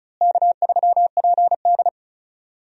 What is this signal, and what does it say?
Morse code, heard as a keyed tone, about 36 wpm, 700 Hz. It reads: K3PD
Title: USA Call Signs - 4 Characters - Repeated 3x - 35wpm